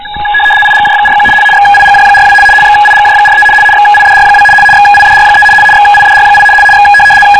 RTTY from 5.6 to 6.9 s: XIDR8 X